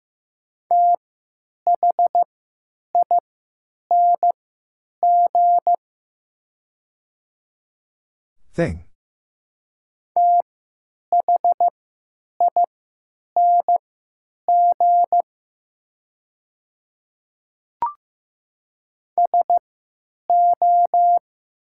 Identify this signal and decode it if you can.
Morse code, heard as a keyed tone, about 15 wpm, spaced as for 9 wpm, 700 Hz. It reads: THING THING SO
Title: 3x Character Spacing - Top 100 Words - 15wpm